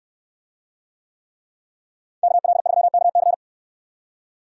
Morse code, heard as a keyed tone, about 69 wpm, 700 Hz. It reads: KL3KC